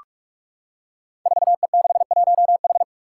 Morse code, heard as a keyed tone, about 45 wpm, 700 Hz. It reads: 4E61H
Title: Five Challenging Letter-Number Combinations 45wpm